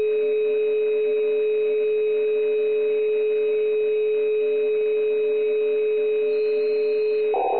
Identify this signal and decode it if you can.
Dial tone (Europe, 425 Hz), with noise.